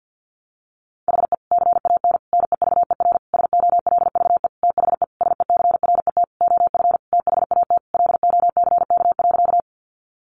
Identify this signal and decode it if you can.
Morse code, heard as a keyed tone, about 50 wpm, 700 Hz. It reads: HE CAN NEVER SOLVE THE SECRET OF THAT LOCK.
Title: Sentences from Top 1300 Words 50wpm